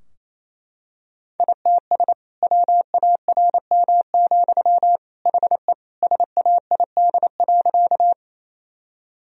Morse code, heard as a keyed tone, about 28 wpm, 700 Hz. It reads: ITS WARM, HE SAID.
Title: Sentences from Top 1000 Words - Review Part 2 - 28wpm